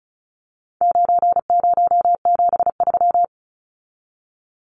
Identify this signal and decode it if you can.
Morse code, heard as a keyed tone, about 34 wpm, 700 Hz. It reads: 9073
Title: Numbers - 4-Digit 35wpm